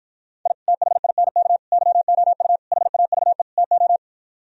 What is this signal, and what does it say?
Morse code, heard as a keyed tone, about 57 wpm, 700 Hz. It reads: I THINK YOU HAVE TO